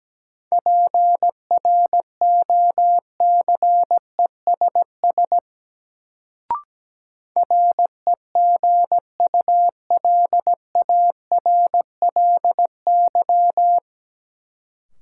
Morse code, heard as a keyed tone, about 17 wpm, 700 Hz. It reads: PROCESS REGULARLY
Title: Sentences from Top 1400 Words - New Word Drill Alt - 17wpm